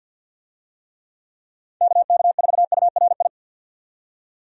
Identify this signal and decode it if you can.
Morse code, heard as a keyed tone, about 45 wpm, 700 Hz. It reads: KK4URI
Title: USA Call Signs - 6 Characters - 50wpm